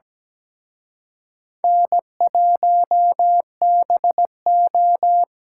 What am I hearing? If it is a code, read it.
Morse code, heard as a keyed tone, about 20 wpm, 700 Hz. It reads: N1BO
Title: Speed Racing - USA 1x2 Call Signs 17wpm